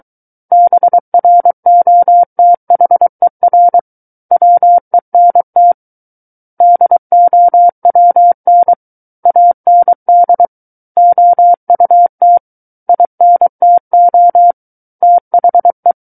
Morse code, heard as a keyed tone, about 23 wpm, 700 Hz. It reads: BROTHER WENT DOWN AND OUT INTO THE
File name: Book One / 14-Chapter 14 - 23 WPM